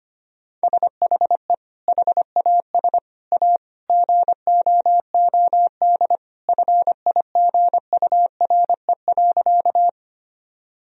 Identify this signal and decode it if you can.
Morse code, heard as a keyed tone, about 25 wpm, 700 Hz. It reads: SHE HAS A GOOD FIGURE.
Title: Sentences from Top 1000 Words - Review Part 2 - 25wpm